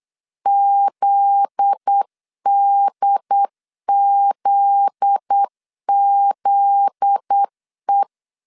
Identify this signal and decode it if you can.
Morse code, heard as a keyed tone, about 8 wpm, 800 Hz. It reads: ZDZZE
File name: LEZIONI / 7 GF9Z / 7 GF9Z 8wpm 3213